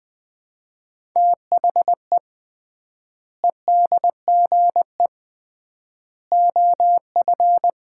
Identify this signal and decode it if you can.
Morse code, heard as a keyed tone, about 20 wpm, 700 Hz. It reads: THE EDGE OF
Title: 3x Word-Spacing - Sets of 3 Words - 20wpm